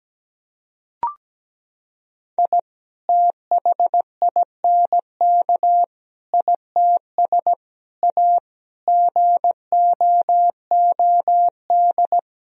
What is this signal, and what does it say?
Morse code, heard as a keyed tone, about 17 wpm, 700 Hz. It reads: I THINK ITS A GOOD
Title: Sentences from Top 1000 Words - Review Part 1 - 17wpm